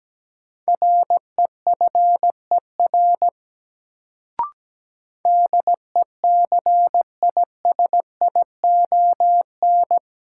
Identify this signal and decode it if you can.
Morse code, heard as a keyed tone, about 17 wpm, 700 Hz. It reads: REFER DECISION
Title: Sentences from Top 1000 Words - New Word Drill - 17wpm